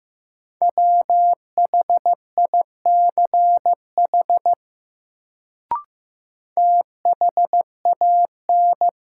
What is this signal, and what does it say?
Morse code, heard as a keyed tone, about 15 wpm, 700 Hz. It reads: WHICH THAN
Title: Sentences from Top 100 Words - New Word Drill - 15wpm